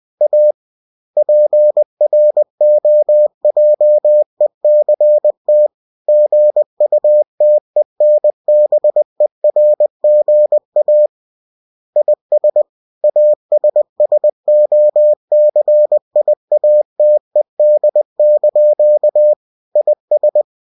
Morse code, heard as a keyed tone, about 20 wpm, 600 Hz. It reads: A PROJECT GUTENBERGA IS ASSOCIATED) IS
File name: cw-books / winnie-the-pooh / winnie-20wpm-page-0115